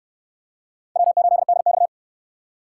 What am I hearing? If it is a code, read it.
Morse code, heard as a keyed tone, about 66 wpm, 700 Hz. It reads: W8DX